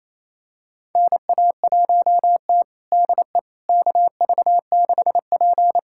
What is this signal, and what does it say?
Morse code, heard as a keyed tone, about 27 wpm, 690 Hz. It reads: NA1T DE KV6P
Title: USA Callsign DE Callsign - 4 Characters - 28wpm